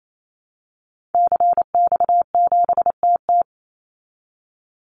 Morse code, heard as a keyed tone, about 27 wpm, 700 Hz. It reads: CX7TT
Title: Call Signs Repeated 3x - 28wpm